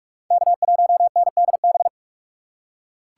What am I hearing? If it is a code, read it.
Morse code, heard as a keyed tone, about 42 wpm, 700 Hz. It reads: K1NDB
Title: USA Call Signs - 6x Character Spacing - 45wpm